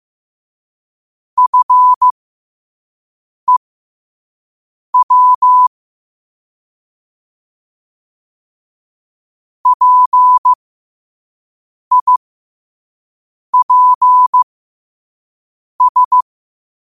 Morse code, heard as a keyed tone, about 15 wpm, spaced as for 5 wpm, 1000 Hz. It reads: FEW PIPS
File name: qotdfiles / 5wpm